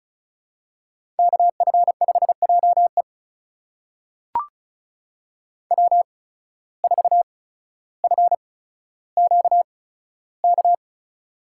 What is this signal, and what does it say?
Morse code, heard as a keyed tone, about 36 wpm, spaced as for 15 wpm, 700 Hz. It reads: KF5JE W4FQK
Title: USA Call Signs - 5 Characters - 8x Character Spacing - 35wpm